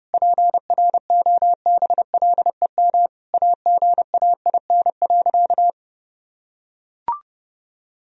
Morse code, heard as a keyed tone, about 32 wpm, 700 Hz. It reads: PROBLEM AGAIN.
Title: Sentences from Top 1000 Words - Review Part 3 - 30wpm